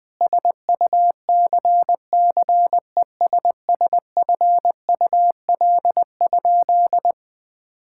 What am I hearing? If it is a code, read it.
Morse code, heard as a keyed tone, about 19 wpm, 700 Hz. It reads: SUCCESSFUL?
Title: Sentences from Top 1000 Words - Review Part 6 - 20wpm